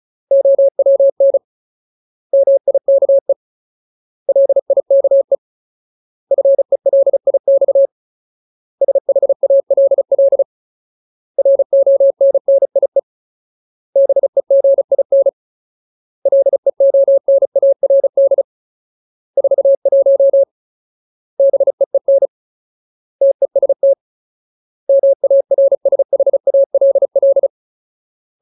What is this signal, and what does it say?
Morse code, heard as a keyed tone, about 34 wpm, 550 Hz. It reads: OWN MIKE LIKE FELIX SHALL RONNIE BEGIN LEONARD 41 BEEN TEST MARSHALL
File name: Ultimate Word List / 35WPM Ultimate